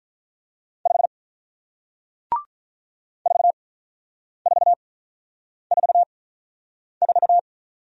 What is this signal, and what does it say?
Morse code, heard as a keyed tone, about 46 wpm, spaced as for 9 wpm, 700 Hz. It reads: 5 4444